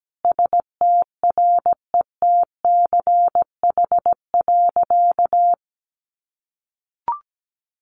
Morse code, heard as a keyed tone, about 17 wpm, 700 Hz. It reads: STRETCH.
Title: Easy Phrases 17wpm